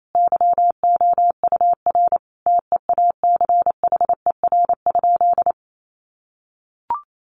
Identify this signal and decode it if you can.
Morse code, heard as a keyed tone, about 28 wpm, 700 Hz. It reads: YOUR TEACHER?
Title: Sentences from Top 1300 Words 28wpm